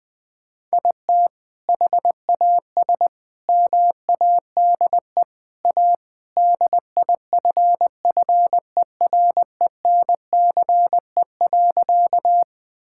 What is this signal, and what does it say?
Morse code, heard as a keyed tone, about 21 wpm, 700 Hz. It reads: IT HAS MADE A DIFFERENCE.
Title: Sentences from Top 1000 Words - Review Part 5 - 20wpm